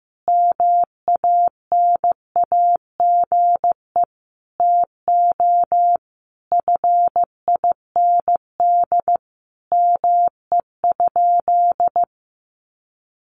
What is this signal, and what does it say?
Morse code, heard as a keyed tone, about 15 wpm, 700 Hz. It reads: MANAGE TO FIND ME?
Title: Sentences from Top 1400 Words 15wpm